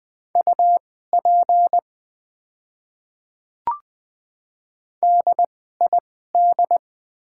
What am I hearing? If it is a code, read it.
Morse code, heard as a keyed tone, about 20 wpm, spaced as for 14 wpm, 700 Hz. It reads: UP DID
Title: 2x Character Spacing - Top 100 Words - 20wpm